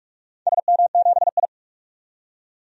Morse code, heard as a keyed tone, about 46 wpm, 700 Hz. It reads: SM7I
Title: DXCC-100 Call Signs - 4 Characters - 45wpm